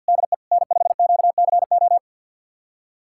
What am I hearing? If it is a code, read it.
Morse code, heard as a keyed tone, about 51 wpm, 700 Hz. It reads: DE N5QCO
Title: USA Callsign DE Callsign - 5 Characters - 50wpm